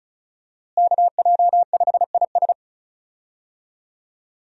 Morse code, heard as a keyed tone, about 36 wpm, 700 Hz. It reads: KJ5IS
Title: USA Call Signs - 5 Characters - Repeated 2x - 35wpm